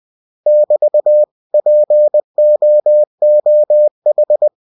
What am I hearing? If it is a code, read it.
Morse code, heard as a keyed tone, about 20 wpm, 600 Hz. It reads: <BT> POOH
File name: cw-books / winnie-the-pooh / winnie-20wpm-page-0081